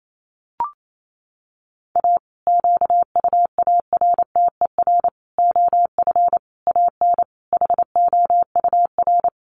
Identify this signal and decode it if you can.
Morse code, heard as a keyed tone, about 28 wpm, 700 Hz. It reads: A QUARTER OF AN HOUR
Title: Sentences from Top 1300 Words 28wpm